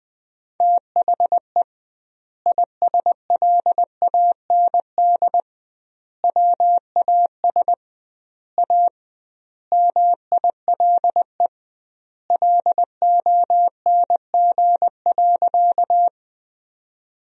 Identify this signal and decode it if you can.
Morse code, heard as a keyed tone, about 20 wpm, 700 Hz. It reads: THE ISLAND WAS A MILE LONG.
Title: 2x Word-Spacing - Sentences from Top 500 Words - 20wpm